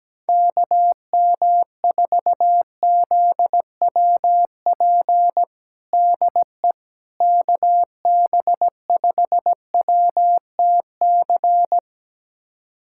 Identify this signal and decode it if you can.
Morse code, heard as a keyed tone, about 17 wpm, 700 Hz. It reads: KM4ZWP DE KB5WTC
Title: USA Callsign DE Callsign 17wpm